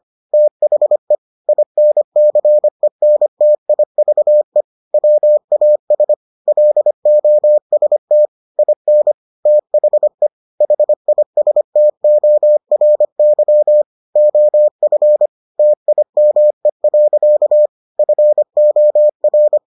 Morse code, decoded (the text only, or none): THE INCENTIVE WAS LOST IN THE HISTORY OF TIME. FOR